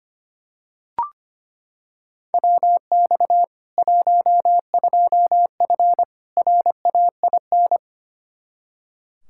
Morse code, heard as a keyed tone, about 25 wpm, 700 Hz. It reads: WX 12F RAIN
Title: K1USN SST QSO Practice 25wpm